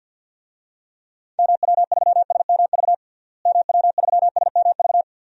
Morse code, heard as a keyed tone, about 52 wpm, 700 Hz. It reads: MW3SMV MW3SMV
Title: DXCC-100 Call Signs - 6 Characters - Repeated 2x - 50wpm